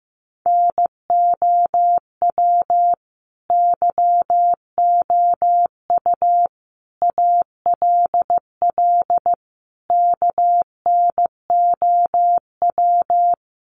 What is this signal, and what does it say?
Morse code, heard as a keyed tone, about 14 wpm, 700 Hz. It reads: NOW YOU ALL KNOW
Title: Sentences from Top 1200 Words 15wpm